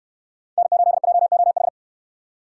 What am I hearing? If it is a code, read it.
Morse code, heard as a keyed tone, about 63 wpm, 700 Hz. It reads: N8JOH